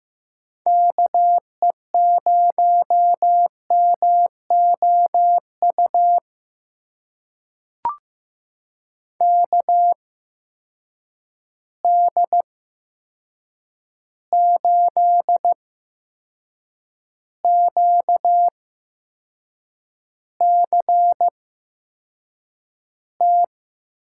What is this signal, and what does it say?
Morse code, heard as a keyed tone, about 15 wpm, spaced as for 6 wpm, 700 Hz. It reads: KE0MOU KD8QCT
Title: USA Call Signs - 6 Characters - 8x Character Spacing - 15wpm